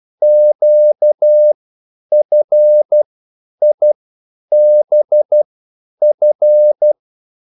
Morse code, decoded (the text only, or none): QFIBF